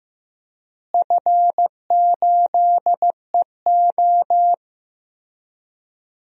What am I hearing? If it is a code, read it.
Morse code, heard as a keyed tone, about 15 wpm, 700 Hz. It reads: F8EO
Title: DXCC-100 Call Signs - 4 Characters - Repeated 3x - 15wpm